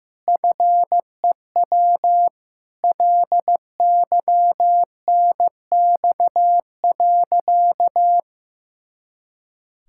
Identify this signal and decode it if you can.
Morse code, heard as a keyed tone, about 15 wpm, 700 Hz. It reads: FEW LYNX.
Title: Pangram Challenge v2 - 15wpm